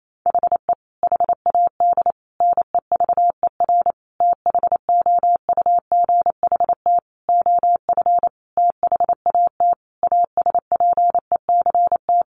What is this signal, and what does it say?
Morse code, heard as a keyed tone, about 28 wpm, 700 Hz. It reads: HE HAD NEVER THOUGHT OF THAT ASPECT
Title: Sentences from Top 1300 Words 28wpm